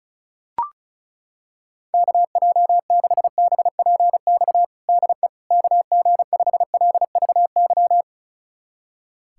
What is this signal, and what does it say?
Morse code, heard as a keyed tone, about 36 wpm, 700 Hz. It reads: KJ6BPX DE KG5LVY
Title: USA Callsign DE Callsign - 6 Characters - 35wpm